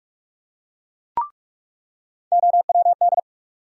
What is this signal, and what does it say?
Morse code, heard as a keyed tone, about 46 wpm, 700 Hz. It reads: OWD
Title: USA Call Sign Postfixes - 3 Letters 45wpm